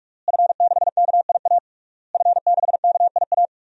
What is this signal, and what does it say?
Morse code, heard as a keyed tone, about 46 wpm, 700 Hz. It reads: F6KIA F6KIA
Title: DXCC-100 Call Signs - 5 Characters - Repeated 3x - 45wpm